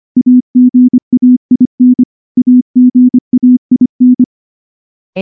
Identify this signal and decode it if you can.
Morse code, heard as a keyed tone, about 24 wpm, 250 Hz. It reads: AGAIN AGAIN
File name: morse-training-25WPM